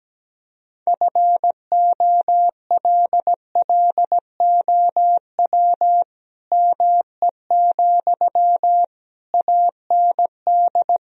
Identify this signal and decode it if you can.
Morse code, heard as a keyed tone, about 17 wpm, 700 Hz. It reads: FOLLOW ME, AND
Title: Sentences from Top 1000 Words - Review Part 3 - 17wpm